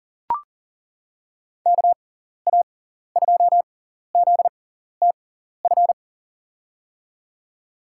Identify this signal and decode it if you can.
Morse code, heard as a keyed tone, about 40 wpm, spaced as for 14 wpm, 700 Hz. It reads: KA2ZTF
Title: USA Call Signs - 6 Characters - 6x Character Spacing - 40wpm